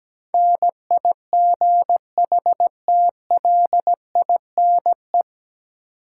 Morse code, heard as a keyed tone, about 17 wpm, 700 Hz. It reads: NIGHTLINE
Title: Sets of 5 Words - New Word Drill - 17wpm